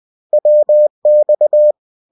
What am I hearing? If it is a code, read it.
Morse code, heard as a keyed tone, about 20 wpm, 600 Hz. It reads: WX